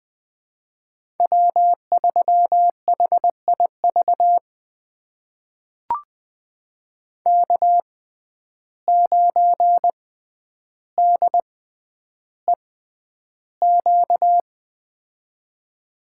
Morse code, heard as a keyed tone, about 20 wpm, spaced as for 10 wpm, 700 Hz. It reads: W3HIV K9DEQ
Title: USA Call Signs - 5 Characters - 6x Character Spacing - 20wpm